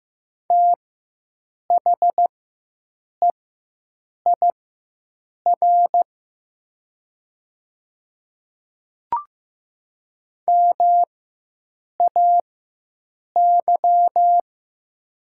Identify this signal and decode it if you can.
Morse code, heard as a keyed tone, about 15 wpm, spaced as for 7 wpm, 700 Hz. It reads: THEIR MAY
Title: 4x Character Spacing - Top 100 Words - 15wpm